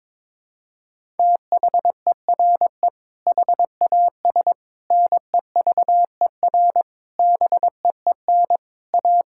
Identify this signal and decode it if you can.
Morse code, heard as a keyed tone, about 22 wpm, 700 Hz. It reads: THERE HAS NEVER BEEN A
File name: practice / 22wpm / Sentences with 8-Word Phrases - 22wpm